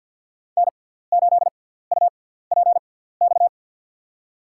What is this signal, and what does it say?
Morse code, heard as a keyed tone, about 52 wpm, spaced as for 17 wpm, 700 Hz. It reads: N8UPX
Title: USA Call Signs - 6x Character Spacing - 50wpm